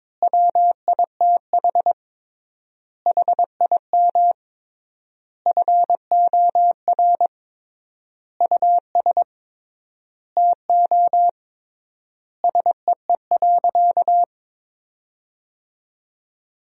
Morse code, heard as a keyed tone, about 22 wpm, 700 Hz. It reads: WITH HIM FOR US TO SEE.
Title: 3x Word-Spacing - Sentences from Top 500 Words - 22wpm